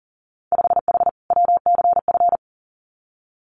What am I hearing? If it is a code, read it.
Morse code, heard as a keyed tone, about 40 wpm, 700 Hz. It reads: 5H WCF
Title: Winter Field Day Exchanges Encore 40wpm